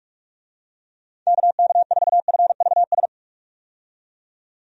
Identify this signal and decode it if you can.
Morse code, heard as a keyed tone, about 43 wpm, 700 Hz. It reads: KK4FVS
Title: USA Call Signs - 6 Characters - 45wpm